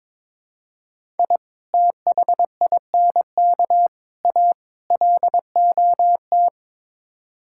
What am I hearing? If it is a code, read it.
Morse code, heard as a keyed tone, about 22 wpm, 700 Hz. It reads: I THINK A LOT
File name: practice / 22wpm / Build A Phrase - 5-Word Phrases - 22wpm